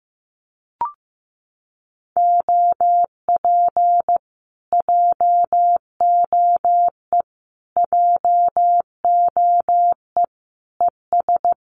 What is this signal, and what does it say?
Morse code, heard as a keyed tone, about 15 wpm, 700 Hz. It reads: OP JOE JOE ES
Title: Common QSO Phrases Encore 15wpm